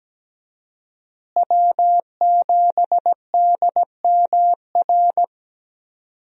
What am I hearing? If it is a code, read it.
Morse code, heard as a keyed tone, about 17 wpm, 700 Hz. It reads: W7DMR
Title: USA Call Signs - 5 Characters - Repeated 3x - 17wpm